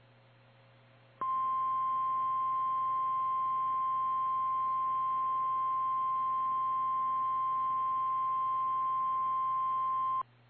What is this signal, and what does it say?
Emergency Alert System (SAME) tones (the 1050 Hz attention tone). It is heard from 1.2 to 10.3 s.
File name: eas / KKPRKXPN / KKPRKXPN